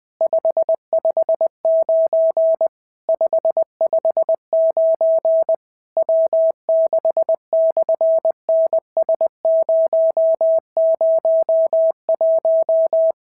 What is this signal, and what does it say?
Morse code, heard as a keyed tone, about 20 wpm, 650 Hz. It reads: 559 559 W6/NS001
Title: SOTA QSO Practice 20wpm